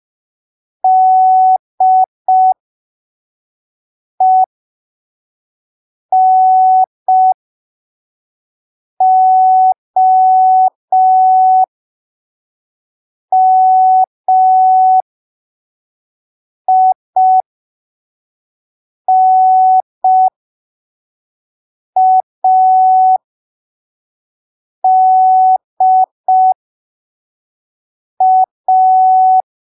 Morse code, heard as a keyed tone, about 5 wpm, spaced as for 3 wpm, 750 Hz. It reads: DENOMINADA